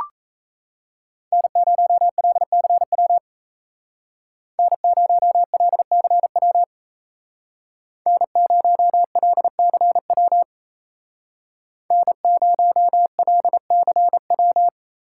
Morse code, heard as a keyed tone, about 36 wpm, 700 Hz. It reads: N0LCW N0LCW N0LCW N0LCW